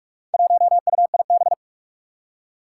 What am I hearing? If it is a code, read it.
Morse code, heard as a keyed tone, about 44 wpm, 700 Hz. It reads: 1UIB
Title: Four Challenging Letter-Number Combinations 45wpm